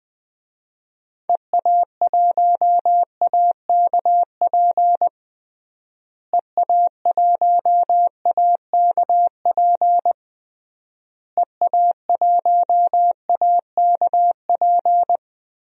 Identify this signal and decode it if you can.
Morse code, heard as a keyed tone, about 20 wpm, 700 Hz. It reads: EA1AKP EA1AKP EA1AKP
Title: DXCC-100 Call Signs - 6 Characters - Repeated 3x - 20wpm